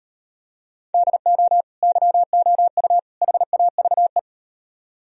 Morse code, heard as a keyed tone, about 36 wpm, 700 Hz. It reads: DO YOU HAVE